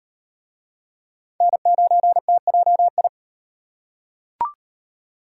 Morse code, heard as a keyed tone, about 38 wpm, 700 Hz. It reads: N9TJI